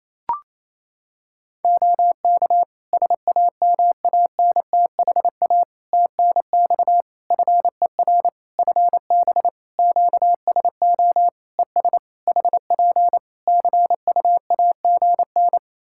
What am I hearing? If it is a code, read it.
Morse code, heard as a keyed tone, about 28 wpm, 700 Hz. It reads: OK SAMANTHA TNX FER FB QSO ES HP CUAGN
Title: Practice QSO - Exchange 4 - Ending - 28wpm